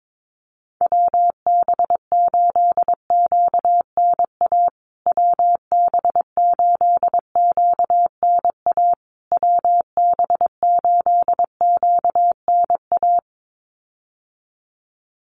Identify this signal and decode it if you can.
Morse code, heard as a keyed tone, about 22 wpm, 700 Hz. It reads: WB8QNA WB8QNA WB8QNA
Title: USA Call Signs Repeated 3x 22wpm